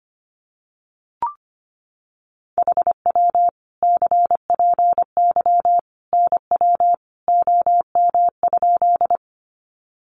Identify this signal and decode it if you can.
Morse code, heard as a keyed tone, about 25 wpm, 700 Hz. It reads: HW CPY NW OM?
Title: Common QSO Phrases 25wpm